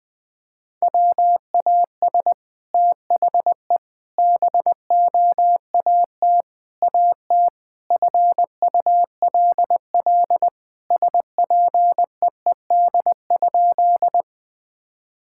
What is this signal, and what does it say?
Morse code, heard as a keyed tone, about 21 wpm, 700 Hz. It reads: WAS THE BOAT AT FULL SPEED?
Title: Sentences from Top 1000 Words - Review Part 5 - 20wpm